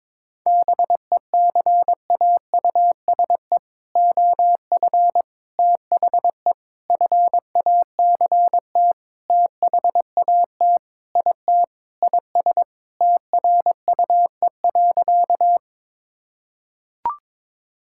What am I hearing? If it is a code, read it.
Morse code, heard as a keyed tone, about 21 wpm, 700 Hz. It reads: BECAUSE OF THE FACT THAT IT IS TRUE.
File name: practice / 22wpm / Sentences with 4-Word Phrases - 22wpm